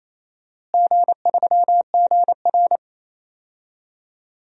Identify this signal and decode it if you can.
Morse code, heard as a keyed tone, about 28 wpm, 700 Hz. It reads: G3GR
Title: DXCC-100 Call Signs - 4 Characters - Repeated 2x - 28wpm